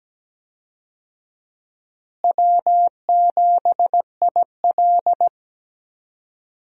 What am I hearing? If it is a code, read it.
Morse code, heard as a keyed tone, about 17 wpm, 700 Hz. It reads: W7IL
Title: USA Call Signs - 4 Characters - Repeated 2x - 17wpm